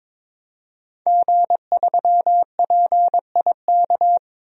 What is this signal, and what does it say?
Morse code, heard as a keyed tone, about 22 wpm, 700 Hz. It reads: G3PIK
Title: DXCC-100 Call Signs - 5 Characters - 22wpm